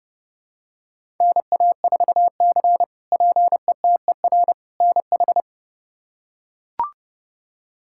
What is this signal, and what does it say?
Morse code, heard as a keyed tone, about 30 wpm, 700 Hz. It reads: NA4C PETER NH
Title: K1USN SST QSO Practice - 30wpm